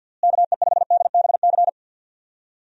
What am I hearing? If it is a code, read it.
Morse code, heard as a keyed tone, about 50 wpm, 700 Hz. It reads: KE5DBC
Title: USA Call Signs - 6 Characters - Repeated 2x - 50wpm